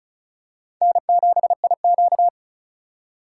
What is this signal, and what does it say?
Morse code, heard as a keyed tone, about 34 wpm, 700 Hz. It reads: N7IQ
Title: USA Call Signs - 2x Character Spacing - 35wpm